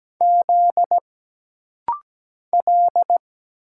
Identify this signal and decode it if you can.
Morse code, heard as a keyed tone, about 17 wpm, 700 Hz. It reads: Z L
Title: Single Letters - Rapid-Fire 17wpm